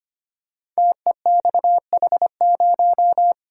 Morse code, heard as a keyed tone, about 25 wpm, 700 Hz. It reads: TEXH0
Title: Code Groups - 5 Letter:Numbers - 25wpm